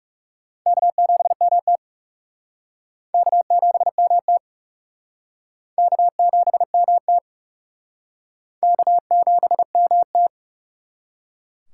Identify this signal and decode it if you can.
Morse code, heard as a keyed tone, about 38 wpm, 700 Hz. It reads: K7MT K7MT K7MT K7MT